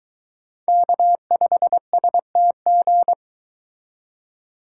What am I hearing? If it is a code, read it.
Morse code, heard as a keyed tone, about 22 wpm, 700 Hz. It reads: K5STG